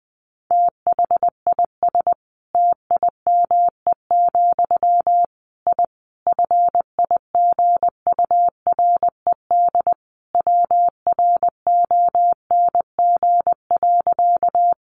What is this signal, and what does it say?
Morse code, heard as a keyed tone, about 20 wpm, 700 Hz. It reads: THIS TIME, I FIGURED WRONG.
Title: A Sound Of Thunder 20wpm